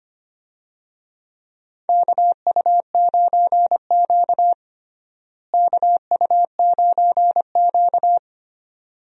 Morse code, heard as a keyed tone, about 25 wpm, 700 Hz. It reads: KU9Q KU9Q
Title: USA Call Signs - 4 Characters - Repeated 2x - 25wpm